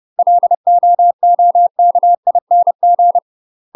Morse code, heard as a keyed tone, about 30 wpm, 700 Hz. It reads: LOOKING
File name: UK MASTER QSOs TRAINING / 30WPM / UK MASTER QSOs 0050